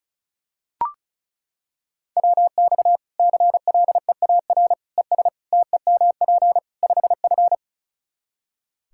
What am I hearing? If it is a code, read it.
Morse code, heard as a keyed tone, about 36 wpm, 700 Hz. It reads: WX CLEAR ES TEMP 5F
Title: Practice QSO - Exchange 2 - Info - 35wpm